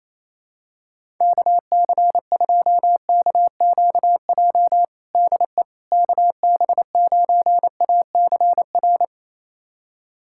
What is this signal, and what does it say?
Morse code, heard as a keyed tone, about 28 wpm, 700 Hz. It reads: KC2KQJ DE KB9ACR
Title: USA Callsign DE Callsign - 6 Characters - 28wpm